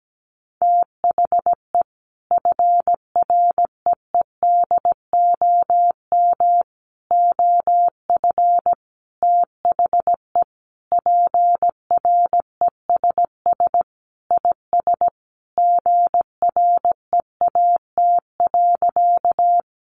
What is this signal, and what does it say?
Morse code, heard as a keyed tone, about 17 wpm, 700 Hz. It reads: THE FREEDOM OF THE PRESS IS GREAT.
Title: Sentences from Top 800 Words 17wpm